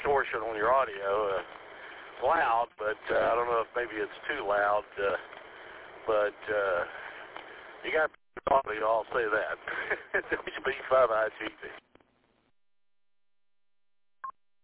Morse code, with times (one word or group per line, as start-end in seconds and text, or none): none